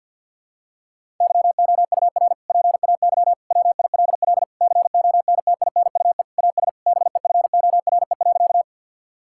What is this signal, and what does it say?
Morse code, heard as a keyed tone, about 48 wpm, 700 Hz. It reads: YOUR PAY WILL CONTINUE AS BEFORE.
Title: Sentences from Top 1000 Words - Review Part 6 - 50wpm